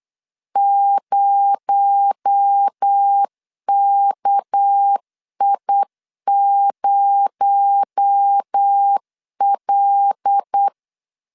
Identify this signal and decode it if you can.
Morse code, heard as a keyed tone, about 8 wpm, 800 Hz. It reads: 0KI0L
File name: LEZIONI / 6 PKX0 / 6 PKX0 8wpm 3341